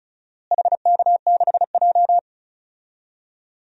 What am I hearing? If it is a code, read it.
Morse code, heard as a keyed tone, about 34 wpm, 700 Hz. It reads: HK6J
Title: DXCC-100 Call Signs - 4 Characters - Repeated 3x - 35wpm